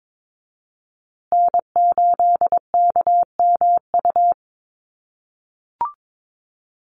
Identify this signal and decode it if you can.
Morse code, heard as a keyed tone, about 22 wpm, 700 Hz. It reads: N8KMU
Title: USA Call Signs Repeated 2x 22wpm